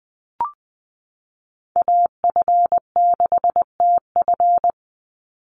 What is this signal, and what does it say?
Morse code, heard as a keyed tone, about 20 wpm, 700 Hz. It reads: AF6TF
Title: USA Call Signs 20wpm